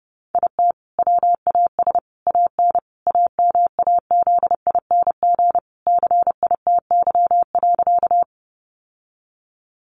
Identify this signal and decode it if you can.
Morse code, heard as a keyed tone, about 30 wpm, 700 Hz. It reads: IT WAS AN AMAZING CITY.
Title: Sentences from Top 1500 Words 30wpm